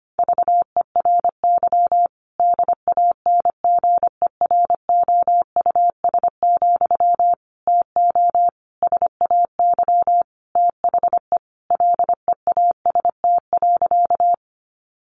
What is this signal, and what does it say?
Morse code, heard as a keyed tone, about 25 wpm, 700 Hz. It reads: VERY DANGEROUS, TO SAY THE LEAST.
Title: Sentences from Top 1200 Words 25wpm